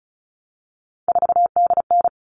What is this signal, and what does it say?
Morse code, heard as a keyed tone, about 34 wpm, 700 Hz. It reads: 4DN
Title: All Cut Numbers - 3-Digit 35wpm